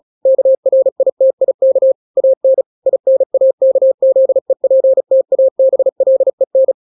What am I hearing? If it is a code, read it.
Morse code, heard as a keyed tone, about 36 wpm, 530 Hz. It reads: KRITIK AN INAKZEPTABLEN
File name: daily / 35wpm